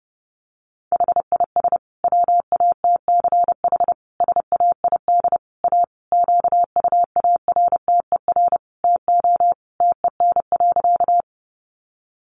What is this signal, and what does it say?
Morse code, heard as a keyed tone, about 29 wpm, 700 Hz. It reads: HIS WATCH SAID A QUARTER TO TEN.
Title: Sentences from Top 1300 Words 30wpm